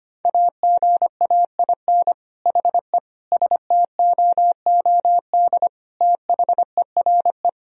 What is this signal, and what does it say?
Morse code, heard as a keyed tone, about 25 wpm, 700 Hz. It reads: AGAIN HE STOOD THERE
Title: Sentences from Top 1000 Words - Review Part 3 - 25wpm